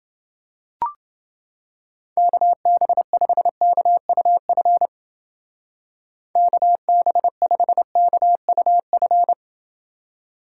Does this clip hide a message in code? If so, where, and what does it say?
Yes: Morse code from 2.2 to 9.3 s, heard as a keyed tone, about 29 wpm, 700 Hz. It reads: KB5KUF KB5KUF